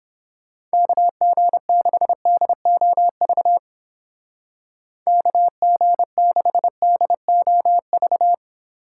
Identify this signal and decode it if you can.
Morse code, heard as a keyed tone, about 29 wpm, 700 Hz. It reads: KG6DOV KG6DOV